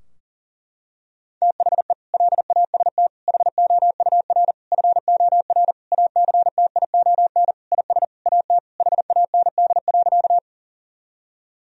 Morse code, heard as a keyed tone, about 40 wpm, 700 Hz. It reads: THE LAST HOUR FOR ACTION IS AT HAND.
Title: Sentences from Top 1000 Words - Review Part 4 - 40wpm